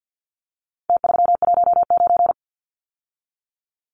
Morse code, heard as a keyed tone, about 50 wpm, 700 Hz. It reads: T319
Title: Numbers - Contest Style - 4-Digit 50wpm